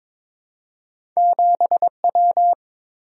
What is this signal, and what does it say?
Morse code, heard as a keyed tone, about 22 wpm, 700 Hz. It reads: 7W
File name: practice / 22wpm / Two Letter-Number Combinations - 22wpm